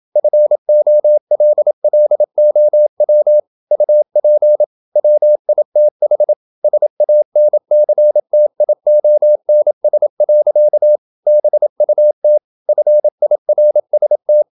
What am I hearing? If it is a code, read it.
Morse code, heard as a keyed tone, about 27 wpm, 600 Hz. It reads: FOLLOW UP WITH SANCTIONS. BUT FIRST
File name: Never 0027wpm